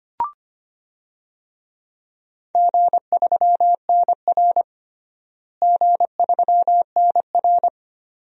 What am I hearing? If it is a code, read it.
Morse code, heard as a keyed tone, about 25 wpm, 700 Hz. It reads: G3NR G3NR
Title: DXCC-100 Call Signs - 4 Characters - Repeated 2x - 25wpm